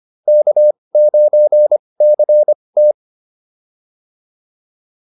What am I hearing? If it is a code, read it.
Morse code, heard as a keyed tone, about 25 wpm, spaced as for 20 wpm, 600 Hz. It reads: K9CT